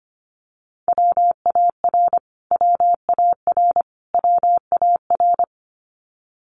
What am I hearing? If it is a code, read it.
Morse code, heard as a keyed tone, about 25 wpm, 700 Hz. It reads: WAR WAR WAR
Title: Top 500 Words Repeated 25wpm